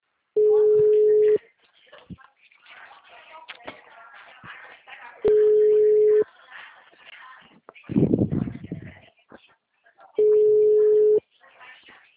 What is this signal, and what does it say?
Ringback tone (Europe, 425 Hz), with speech.